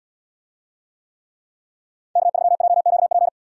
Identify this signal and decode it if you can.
Morse code, heard as a keyed tone, about 78 wpm, 700 Hz. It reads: K3JQP